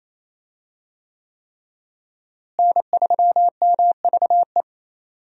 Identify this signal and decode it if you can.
Morse code, heard as a keyed tone, about 28 wpm, 700 Hz. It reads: N3MVE
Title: USA Call Signs - 5 Characters - Repeated 3x - 28wpm